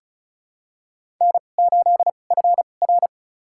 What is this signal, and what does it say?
Morse code, heard as a keyed tone, about 36 wpm, spaced as for 25 wpm, 700 Hz. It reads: N8FR
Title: USA Call Signs - 4 Characters - 2x Character Spacing - 35wpm